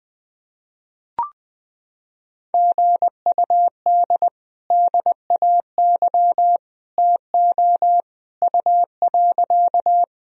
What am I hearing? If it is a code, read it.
Morse code, heard as a keyed tone, about 21 wpm, 700 Hz. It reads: GUD DAY TO U.
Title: Sentences from Top 200 Words and Abbreviations 20wpm